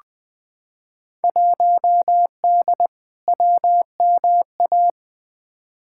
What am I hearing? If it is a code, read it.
Morse code, heard as a keyed tone, about 20 wpm, 700 Hz. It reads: 1D WMA
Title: Field Day Exchanges - 20wpm